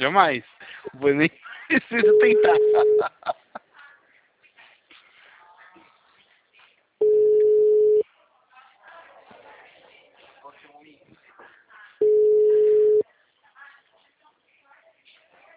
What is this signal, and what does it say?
Ringback tone (Europe, 425 Hz), with speech.